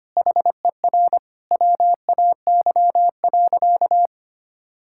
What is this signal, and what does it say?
Morse code, heard as a keyed tone, about 25 wpm, 700 Hz. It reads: HER WAY.
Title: Sentences from Top 1000 Words - Review Part 5 - 25wpm